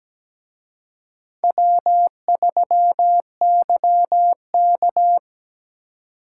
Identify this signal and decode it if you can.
Morse code, heard as a keyed tone, about 20 wpm, 700 Hz. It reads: W3YK
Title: Speed Racing - USA Call Signs 17wpm